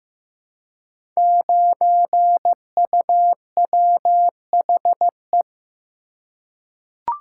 Morse code, heard as a keyed tone, about 15 wpm, 700 Hz. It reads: 9UWHE
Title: Code Groups - 5 Letter:Numbers - 15wpm